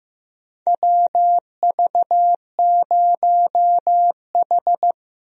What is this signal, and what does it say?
Morse code, heard as a keyed tone, about 15 wpm, 700 Hz. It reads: WV0H
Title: USA Call Signs - 4 Characters - Repeated 3x - 15wpm